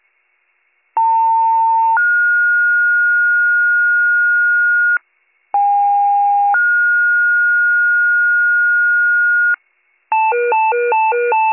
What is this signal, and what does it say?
Two-tone page, heard as a one-second tone, then a three-second tone, about 900 then 1430 Hz, about 810 then 1430 Hz. It is heard from 1.0 to 9.6 s.